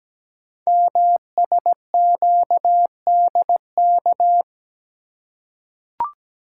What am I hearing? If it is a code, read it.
Morse code, heard as a keyed tone, about 17 wpm, 700 Hz. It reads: MSQDK
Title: Code Groups - 5 Letters - 17wpm